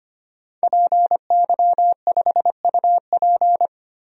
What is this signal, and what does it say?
Morse code, heard as a keyed tone, about 25 wpm, 700 Hz. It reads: PY5UP
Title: DXCC-100 Call Signs - 5 Characters - Repeated 3x - 25wpm